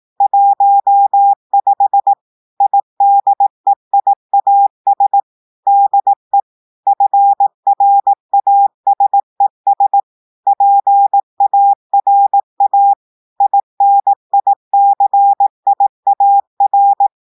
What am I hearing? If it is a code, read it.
Morse code, heard as a keyed tone, about 18 wpm, 800 Hz. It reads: 15 IDEIAS DE FRASES PARA INICIAR
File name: portugais texte en clair 2 vitesse 18 mots minutes